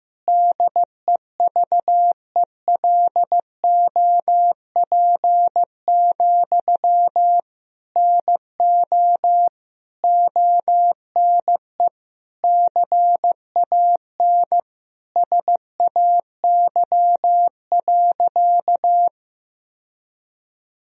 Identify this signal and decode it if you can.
Morse code, heard as a keyed tone, about 14 wpm, 700 Hz. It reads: DEVELOP, NO ONE CAN SAY.
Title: Sentences from Top 1000 Words - Review Part 2 - 15wpm